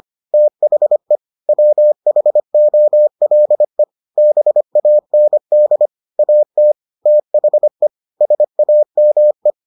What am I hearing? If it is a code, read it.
Morse code, heard as a keyed tone, about 25 wpm, 600 Hz. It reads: THE WHOLE BAND AT THE SAME